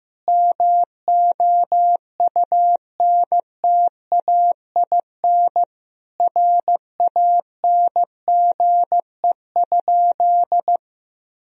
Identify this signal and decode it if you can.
Morse code, heard as a keyed tone, about 16 wpm, 700 Hz. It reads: MOUNTAIN RANGE?
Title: Sentences from Top 1000 Words - Review Part 4 - 15wpm